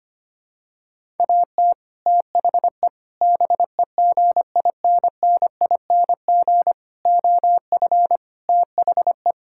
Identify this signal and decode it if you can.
Morse code, heard as a keyed tone, about 25 wpm, 700 Hz. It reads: AT THE BEGINNING OF THE
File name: practice / 25wpm / Sentences with 3-Word Phrases - 25wpm